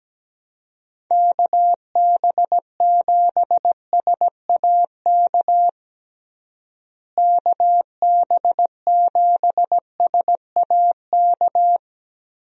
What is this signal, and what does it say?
Morse code, heard as a keyed tone, about 17 wpm, 700 Hz. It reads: KB7SAK KB7SAK
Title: USA Call Signs - 6 Characters - Repeated 2x - 17wpm